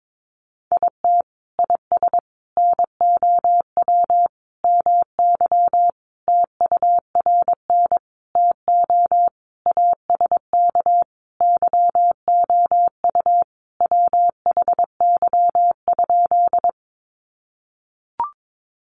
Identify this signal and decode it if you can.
Morse code, heard as a keyed tone, about 22 wpm, 700 Hz. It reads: IT IS NOW MY TURN TO ASK YOU WHY?
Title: Sentences from Top 200 Words 22wpm